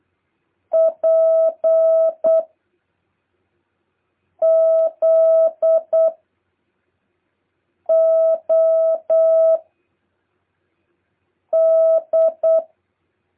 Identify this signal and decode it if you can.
Morse code, heard as a keyed tone, about 8 wpm, 650 Hz. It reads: P Z O D
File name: jh8cbh / 2023cw